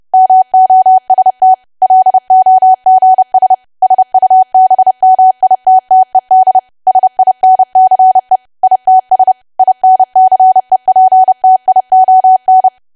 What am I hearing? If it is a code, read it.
Morse code, heard as a keyed tone, about 30 wpm, 750 Hz. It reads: MOST LOGS SUBMITTED SINCE ITS INCEPTION
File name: cwa / W1AW-Level3 / 30WPM